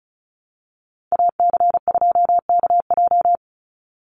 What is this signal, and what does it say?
Morse code, heard as a keyed tone, about 34 wpm, 700 Hz. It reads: AC2KJ
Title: Call Signs Repeated 2x - 35wpm